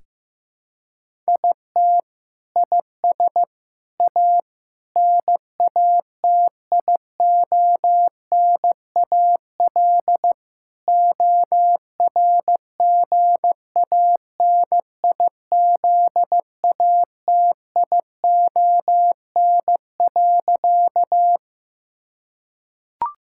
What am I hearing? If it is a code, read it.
Morse code, heard as a keyed tone, about 15 wpm, 700 Hz. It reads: IT IS A NATIONAL ORGANIZATION.
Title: Sentences from Top 1000 Words - Review Part 5 - 15wpm